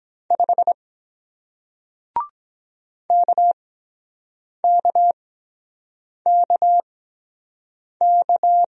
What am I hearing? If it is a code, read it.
Morse code, heard as a keyed tone, about 23 wpm, spaced as for 7 wpm, 700 Hz. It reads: 5 KKKK